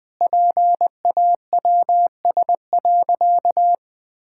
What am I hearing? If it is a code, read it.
Morse code, heard as a keyed tone, about 21 wpm, 700 Hz. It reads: PAWS.
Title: All-Age Appropriate Jokes - 20wpm